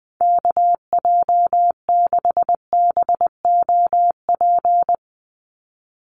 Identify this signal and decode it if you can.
Morse code, heard as a keyed tone, about 20 wpm, 700 Hz. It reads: KJ6BOP
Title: USA Call Signs 20wpm